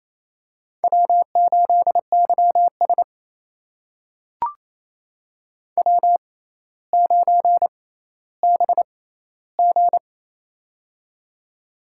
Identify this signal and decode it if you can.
Morse code, heard as a keyed tone, about 29 wpm, spaced as for 15 wpm, 700 Hz. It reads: W8YS W9BG
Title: USA Call Signs - 4 Characters - 6x Character Spacing - 28wpm